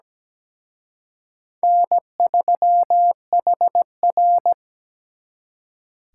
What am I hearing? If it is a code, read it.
Morse code, heard as a keyed tone, about 20 wpm, 700 Hz. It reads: N3HR
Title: Speed Racing - USA 1x2 Call Signs 17wpm